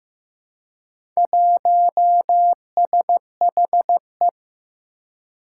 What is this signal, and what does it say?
Morse code, heard as a keyed tone, about 15 wpm, 700 Hz. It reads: 1SHE
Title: Four Challenging Letter-Number Combinations 15wpm